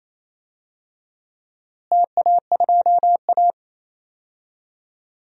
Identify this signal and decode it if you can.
Morse code, heard as a keyed tone, about 28 wpm, 700 Hz. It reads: TA2A
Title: DXCC-100 Call Signs - 4 Characters - Repeated 2x - 28wpm